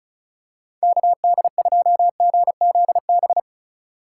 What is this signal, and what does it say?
Morse code, heard as a keyed tone, about 36 wpm, 690 Hz. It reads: KD2GZB
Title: USA Call Signs - 6 Characters - 4x Character Spacing - 35wpm